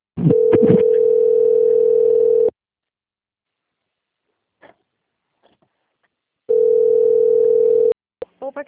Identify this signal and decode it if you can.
Ringback tone (North America), 440 + 480 Hz.
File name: out- - - - - .7752